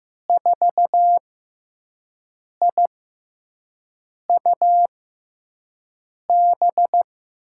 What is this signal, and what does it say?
Morse code, heard as a keyed tone, about 15 wpm, spaced as for 5 wpm, 700 Hz. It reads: 4IUB
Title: USA Call Signs - 6x Character Spacing - 15wpm